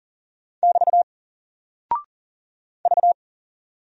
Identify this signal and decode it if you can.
Morse code, heard as a keyed tone, about 40 wpm, 700 Hz. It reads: <BT> V
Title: Single Character (Letter:Number:Punctuation:Prosign) 40wpm